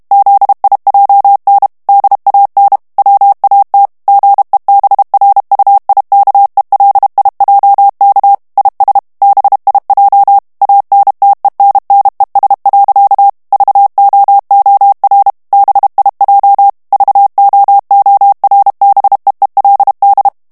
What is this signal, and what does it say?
Morse code, heard as a keyed tone, about 32 wpm, 800 Hz. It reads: ZIJN DAN WAT GEBRUIKELIJK IS BIJ ANTENNES. VOOR BIJ VOORBEELD
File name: HOLLANDAIS texte clair 1 vitesse 30 mots minute